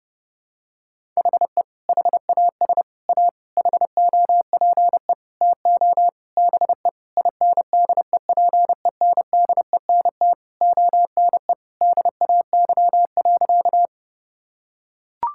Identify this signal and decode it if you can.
Morse code, heard as a keyed tone, about 30 wpm, 700 Hz. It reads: HE HAS A HOPE TO BE INDEPENDENT ONE DAY.
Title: Sentences from Top 1000 Words - Review Part 6 - 30wpm